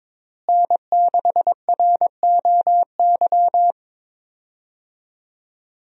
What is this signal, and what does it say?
Morse code, heard as a keyed tone, about 22 wpm, 700 Hz. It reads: N6ROY
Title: USA Call Signs - 5 Characters - Repeated 3x - 22wpm